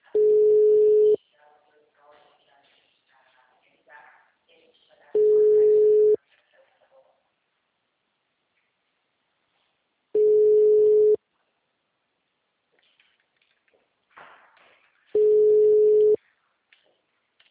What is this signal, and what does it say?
Ringback tone (Europe, 425 Hz), with speech.
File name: out- -2005- - - .11